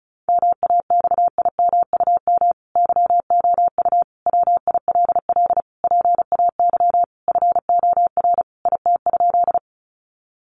Morse code, heard as a keyed tone, about 34 wpm, 700 Hz. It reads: MAXIMUM YOU WILL PAY FOR IT?
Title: Sentences from Top 1300 Words 35wpm